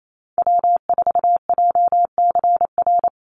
Morse code, heard as a keyed tone, about 28 wpm, 700 Hz. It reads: W4JCR
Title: USA Call Signs 28wpm